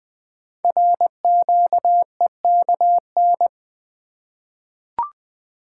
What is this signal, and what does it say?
Morse code, heard as a keyed tone, about 20 wpm, 700 Hz. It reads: RQEKN
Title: Code Groups - 5 Letters - 20wpm